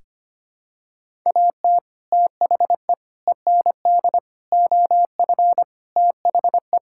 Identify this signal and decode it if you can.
Morse code, heard as a keyed tone, about 25 wpm, 700 Hz. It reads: AT THE END OF THE
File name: practice / 25wpm / Sentences with 3-Word Phrases - 25wpm